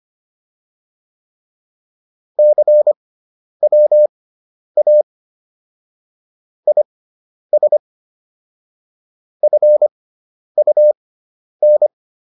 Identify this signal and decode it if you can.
Morse code, heard as a keyed tone, about 25 wpm, spaced as for 10 wpm, 600 Hz. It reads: CWA IS FUN